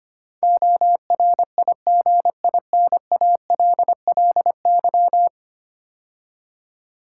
Morse code, heard as a keyed tone, about 25 wpm, 700 Hz. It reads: ORIGINALLY
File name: 10 Letter Words - First 3 Letters Given - 25wpm